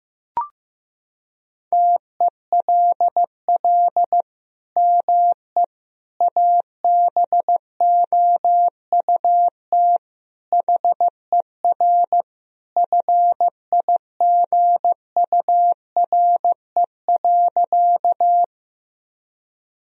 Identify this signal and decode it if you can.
Morse code, heard as a keyed tone, about 15 wpm, 700 Hz. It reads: TELL ME ABOUT HER FIGURE.
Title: Sentences from Top 1000 Words - Review Part 2 - 15wpm